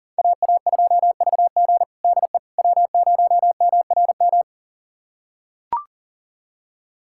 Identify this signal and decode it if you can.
Morse code, heard as a keyed tone, about 40 wpm, 700 Hz. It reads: AA2VG DE W0MRM
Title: USA Callsign DE Callsign - 5 Characters - 40wpm